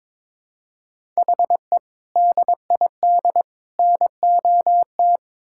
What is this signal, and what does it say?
Morse code, heard as a keyed tone, about 22 wpm, 700 Hz. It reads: HE DID NOT
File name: Build A Phrase - 5-Word Phrases - 22wpm